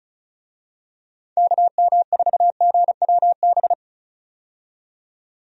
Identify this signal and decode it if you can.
Morse code, heard as a keyed tone, about 36 wpm, 700 Hz. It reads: KM4GWB
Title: USA Call Signs - 6 Characters - 35wpm